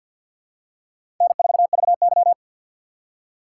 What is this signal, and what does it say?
Morse code, heard as a keyed tone, about 52 wpm, 700 Hz. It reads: N4VY
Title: USA Call Signs - 6x Character Spacing - 50wpm